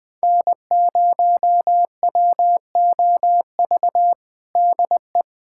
Morse code, heard as a keyed tone, about 20 wpm, 700 Hz. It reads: N0WOV DE
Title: USA Callsign DE Callsign - 5 Characters - 20wpm